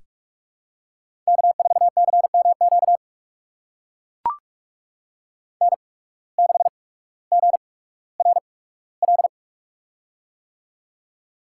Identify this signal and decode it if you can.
Morse code, heard as a keyed tone, about 44 wpm, spaced as for 19 wpm, 700 Hz. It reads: K4CMQ N6GRL
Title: USA Call Signs - 8x Character Spacing - 45wpm